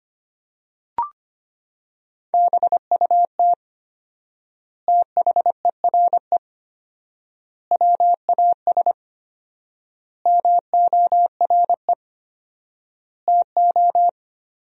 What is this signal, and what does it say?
Morse code, heard as a keyed tone, about 25 wpm, 700 Hz. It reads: BUT THERE WAS MORE TO
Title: 4x Word-Spacing - Sentences from Top 100 Words - 25wpm